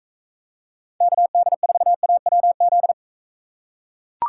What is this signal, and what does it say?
Morse code, heard as a keyed tone, about 42 wpm, 700 Hz. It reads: KD4AWZ